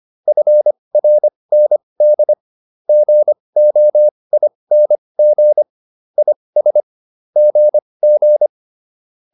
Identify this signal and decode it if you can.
Morse code, heard as a keyed tone, about 25 wpm, spaced as for 20 wpm, 600 Hz. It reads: FRND GOING IS GG